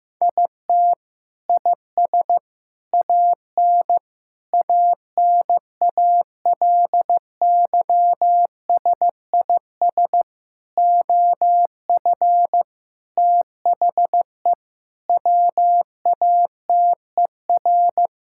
Morse code, heard as a keyed tone, about 15 wpm, 700 Hz. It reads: IT IS AN ANALYSIS OF THE WATER
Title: Sentences from Top 1000 Words - Review Part 5 - 15wpm